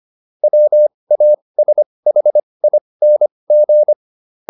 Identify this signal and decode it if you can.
Morse code, heard as a keyed tone, about 25 wpm, spaced as for 20 wpm, 600 Hz. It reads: WASHING